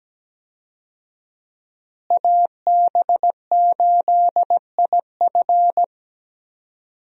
Morse code, heard as a keyed tone, about 17 wpm, 700 Hz. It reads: AB8IF
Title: USA Call Signs - 5 Characters - Repeated 2x - 17wpm